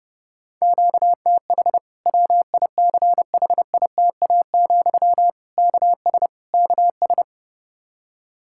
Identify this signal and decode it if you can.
Morse code, heard as a keyed tone, about 30 wpm, 700 Hz. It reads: QTH WICHITA, KS KS
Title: K1USN SST QSO Practice 30wpm